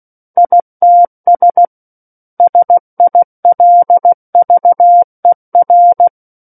Morse code, heard as a keyed tone, about 16 wpm, 700 Hz. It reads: ITS SILVER
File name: Book One / 07-Chapter 7 - 16 WPM